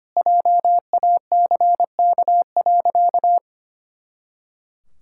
Morse code, heard as a keyed tone, about 25 wpm, 700 Hz. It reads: JACK.
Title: Pangram Challenge v2 - 25wpm